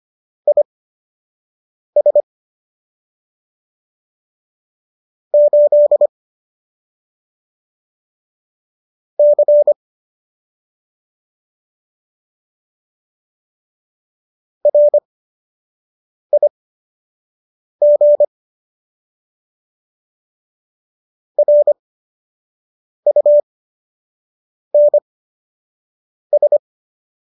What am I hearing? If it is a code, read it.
Morse code, heard as a keyed tone, about 25 wpm, spaced as for 6 wpm, 590 Hz. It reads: IS 8 C RIG RUNS
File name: qso003 06